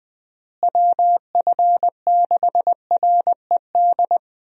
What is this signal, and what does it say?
Morse code, heard as a keyed tone, about 20 wpm, 700 Hz. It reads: WF6RED
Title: USA Call Signs - 6 Characters - Repeated 3x - 20wpm